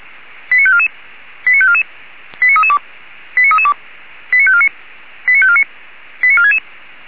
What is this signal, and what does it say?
Five-tone selective call (ZVEI), heard as quick runs of five tones, tones of about 70 ms. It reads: 89544 89544 89332 89332 89549 89549 89466